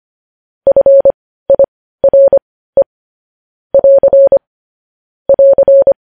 Morse code, heard as a keyed tone, about 25 wpm, spaced as for 15 wpm, 550 Hz. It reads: FIRE <AR> <AR>
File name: podcast / News headlines-1x-25x15